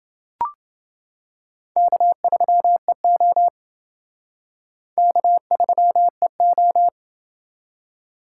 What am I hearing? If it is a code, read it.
Morse code, heard as a keyed tone, about 28 wpm, 700 Hz. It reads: K3EO K3EO